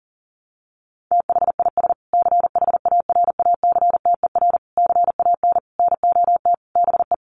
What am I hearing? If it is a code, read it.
Morse code, heard as a keyed tone, about 40 wpm, 700 Hz. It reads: THIS CHARACTER CAN NOT BE
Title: Sets of 5 Words Encore2 40wpm